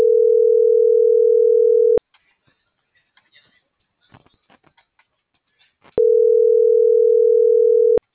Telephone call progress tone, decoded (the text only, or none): ringback tone